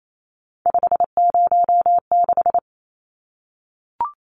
Morse code, heard as a keyed tone, about 28 wpm, 700 Hz. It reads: 506
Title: Numbers - 3-Digit 28wpm